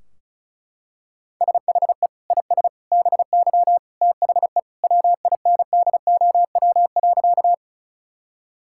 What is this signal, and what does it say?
Morse code, heard as a keyed tone, about 34 wpm, 700 Hz. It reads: SHE IS BY THE WINDOW.
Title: Sentences from Top 1600 Words 35wpm